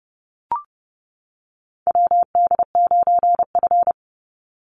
Morse code, heard as a keyed tone, about 30 wpm, 700 Hz. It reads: WD9F
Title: NA SOTA Chaser Callsigns 30wpm